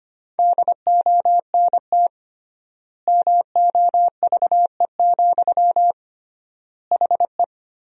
Morse code, heard as a keyed tone, about 25 wpm, 700 Hz. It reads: DONT MOVE, HE
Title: 3x Word-Spacing - Sentences from Top 300 Words - 25wpm